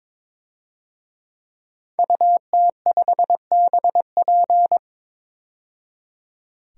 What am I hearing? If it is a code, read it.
Morse code, heard as a keyed tone, about 22 wpm, 700 Hz. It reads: UT5BP
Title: DXCC-100 Call Signs - 5 Characters - 22wpm